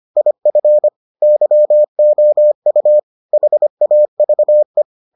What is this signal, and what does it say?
Morse code, heard as a keyed tone, about 25 wpm, 600 Hz. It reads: IF YOU HAVE